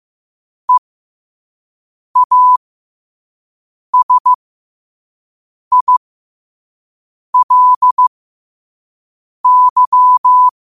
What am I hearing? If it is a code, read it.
Morse code, heard as a keyed tone, about 15 wpm, spaced as for 5 wpm, 1000 Hz. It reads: EASILY
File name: qotdfiles / 5wpm